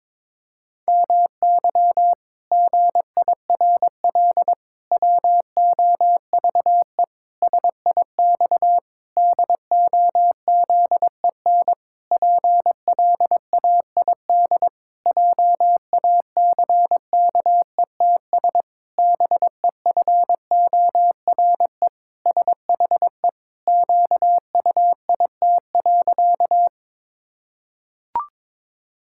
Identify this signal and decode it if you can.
Morse code, heard as a keyed tone, about 22 wpm, 700 Hz. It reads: MY GIRL WOVE SIX DOZEN PLAID JACKETS BEFORE SHE QUIT.
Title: Pangram Challenge v2 - 22wpm